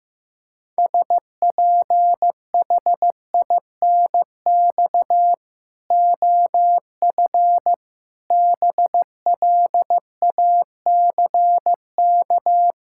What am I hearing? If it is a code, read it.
Morse code, heard as a keyed tone, about 15 wpm, 700 Hz. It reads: SPHINX OF BLACK
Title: Pangram Challenge v2 - 15wpm